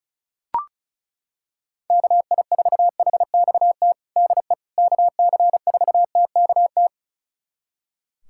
Morse code, heard as a keyed tone, about 36 wpm, 700 Hz. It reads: KI4HXT DE KC4TKT
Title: USA Callsign DE Callsign - 6 Characters - 35wpm